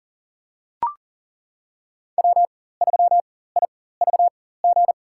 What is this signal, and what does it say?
Morse code, heard as a keyed tone, about 40 wpm, spaced as for 19 wpm, 700 Hz. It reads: W3IVG
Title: USA Call Signs - 5 Characters - 4x Character Spacing - 40wpm